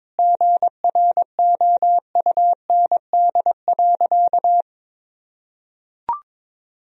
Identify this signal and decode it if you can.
Morse code, heard as a keyed tone, about 22 wpm, 700 Hz. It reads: GROUND.
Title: Sentences from Top 1000 Words - Review Part 2 - 22wpm